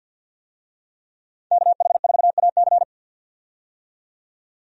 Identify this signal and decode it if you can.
Morse code, heard as a keyed tone, about 50 wpm, 700 Hz. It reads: KH4AC
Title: USA Call Signs - 5 Characters - 50wpm